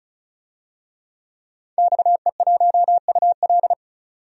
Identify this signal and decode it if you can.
Morse code, heard as a keyed tone, about 36 wpm, 700 Hz. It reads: XE1UL
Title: DXCC-100 Call Signs - 5 Characters - 35wpm